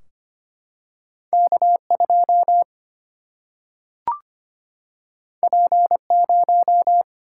Morse code, heard as a keyed tone, about 25 wpm, 700 Hz. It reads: K 2 P 0
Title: Single-Letter Single-Number 25wpm